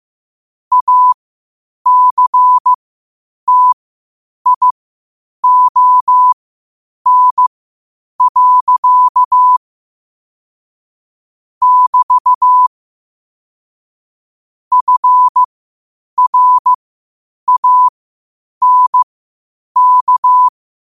Morse code, heard as a keyed tone, about 15 wpm, spaced as for 8 wpm, 1000 Hz. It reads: ACTION. <BT> FRANK
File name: qotdfiles / 8wpm